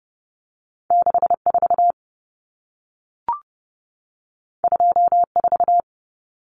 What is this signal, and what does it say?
Morse code, heard as a keyed tone, about 30 wpm, 700 Hz. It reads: 64 24
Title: Numbers - 2-Digit 30wpm